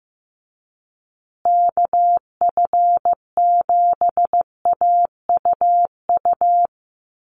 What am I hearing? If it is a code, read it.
Morse code, heard as a keyed tone, about 15 wpm, 700 Hz. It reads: KF7AUU
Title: USA Call Signs Repeated 3x 15wpm